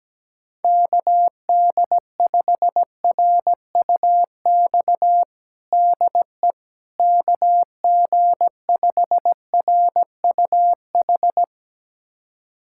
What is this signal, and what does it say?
Morse code, heard as a keyed tone, about 17 wpm, 700 Hz. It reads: KD5RUX DE KG5RUH
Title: USA Callsign DE Callsign - 6 Characters - 17wpm